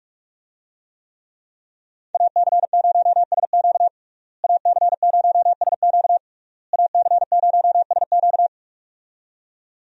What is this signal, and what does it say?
Morse code, heard as a keyed tone, about 42 wpm, 700 Hz. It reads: AC0SQ AC0SQ AC0SQ
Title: USA Call Signs - 5 Characters - Repeated 3x - 45wpm